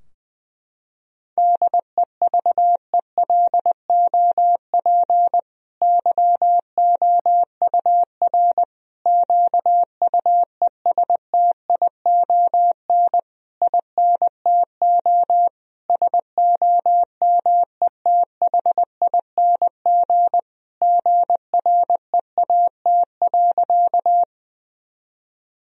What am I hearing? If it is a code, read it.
Morse code, heard as a keyed tone, about 20 wpm, 700 Hz. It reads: DEVELOP YOUR QUESTION INTO SOMETHING GREAT.
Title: Sentences from Top 1000 Words - Review Part 2 - 20wpm